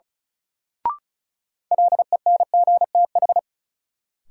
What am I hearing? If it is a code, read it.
Morse code, heard as a keyed tone, about 36 wpm, 700 Hz. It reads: LENGTH
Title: Sentences from Top 1000 Words - New Word Drill - 35wpm